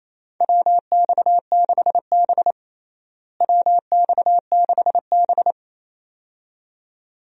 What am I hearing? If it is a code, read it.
Morse code, heard as a keyed tone, about 28 wpm, 700 Hz. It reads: WX6B WX6B
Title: USA Call Signs - 4 Characters - Repeated 2x - 28wpm